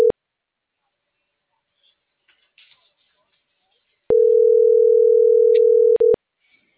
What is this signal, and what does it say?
Ringback tone (North America), 440 + 480 Hz.